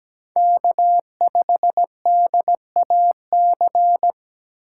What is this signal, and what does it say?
Morse code, heard as a keyed tone, about 17 wpm, 700 Hz. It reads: K5DAC
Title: USA Call Signs - 5 Characters - 3x Character Spacing - 17wpm